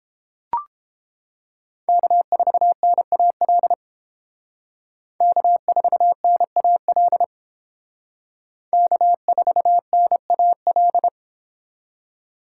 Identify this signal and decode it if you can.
Morse code, heard as a keyed tone, about 31 wpm, 700 Hz. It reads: K4NAL K4NAL K4NAL